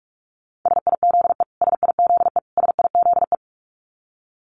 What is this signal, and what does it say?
Morse code, heard as a keyed tone, about 44 wpm, 700 Hz. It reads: SIZE SIZE SIZE
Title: Top 500 Words Repeated 45wpm